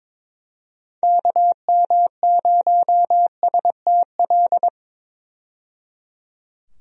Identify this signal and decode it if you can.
Morse code, heard as a keyed tone, about 21 wpm, 700 Hz. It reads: KM0STL
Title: USA Call Signs - 6 Characters - 22wpm